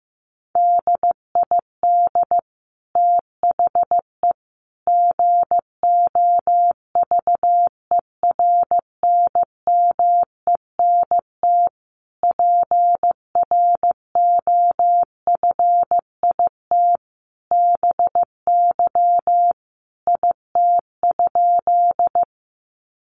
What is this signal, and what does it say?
Morse code, heard as a keyed tone, about 15 wpm, 700 Hz. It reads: DID THE GOVERNMENT PROFIT BY IT?
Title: Sentences from Top 800 Words 15wpm